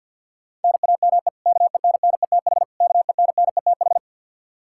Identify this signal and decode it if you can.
Morse code, heard as a keyed tone, about 52 wpm, 700 Hz. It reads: NAME KENNETH KENNETH
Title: Practice QSO - Exchange 1 - Beginning - 50wpm